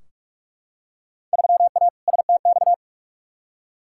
Morse code, heard as a keyed tone, about 46 wpm, 700 Hz. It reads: 3A STX
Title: Field Day Exchanges - 45wpm